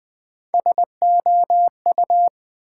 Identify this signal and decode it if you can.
Morse code, heard as a keyed tone, about 20 wpm, 700 Hz. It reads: SOU
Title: Build A Word - 3 Letters At A Time - 8-Letter Words - 20wpm